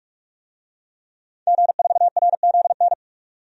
Morse code, heard as a keyed tone, about 46 wpm, 700 Hz. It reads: G4RZN
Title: DXCC-100 Call Signs - 5 Characters - 45wpm